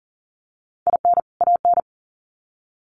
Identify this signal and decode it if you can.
Morse code, heard as a keyed tone, about 40 wpm, 700 Hz. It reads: IN AN
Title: Sets of 2 Words 40wpm